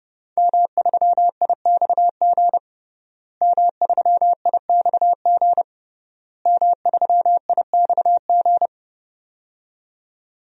Morse code, heard as a keyed tone, about 30 wpm, 700 Hz. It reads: M3IXG M3IXG M3IXG
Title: DXCC-100 Call Signs - 5 Characters - Repeated 3x - 30wpm